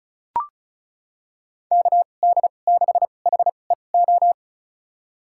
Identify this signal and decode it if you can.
Morse code, heard as a keyed tone, about 34 wpm, spaced as for 25 wpm, 700 Hz. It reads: KD6HEO
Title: USA Call Signs - 6 Characters - 2x Character Spacing - 35wpm